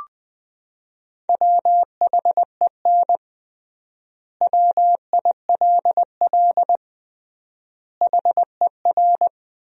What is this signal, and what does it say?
Morse code, heard as a keyed tone, about 20 wpm, 700 Hz. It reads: WHEN WILL HER
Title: 3x Word-Spacing - Sentences from Top 200 Words - 20wpm